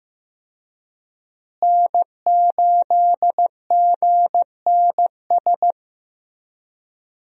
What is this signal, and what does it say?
Morse code, heard as a keyed tone, about 15 wpm, 700 Hz. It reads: N8GNS
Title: USA Call Signs - 5 Characters - Repeated 2x - 15wpm